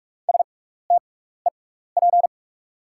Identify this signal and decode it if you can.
Morse code, heard as a keyed tone, about 43 wpm, spaced as for 15 wpm, 700 Hz. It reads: STEP
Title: 6x Character Spacing - Top 500 Words - 45wpm